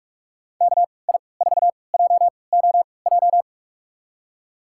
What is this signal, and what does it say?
Morse code, heard as a keyed tone, about 46 wpm, spaced as for 25 wpm, 700 Hz. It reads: KI4JOJ
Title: USA Call Signs - 6 Characters - 3x Character Spacing - 45wpm